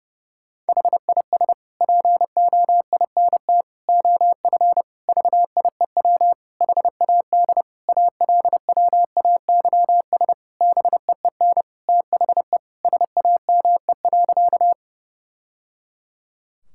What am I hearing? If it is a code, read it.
Morse code, heard as a keyed tone, about 29 wpm, 700 Hz. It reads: HIS POINT OF VIEW HAD ALWAYS BEEN THE SAME.
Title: Sentences from Top 1000 Words - Review Part 6 - 30wpm